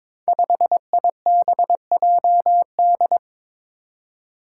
Morse code, heard as a keyed tone, about 22 wpm, 700 Hz. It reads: 5IBJD
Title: Five Challenging Letter-Number Combinations 22wpm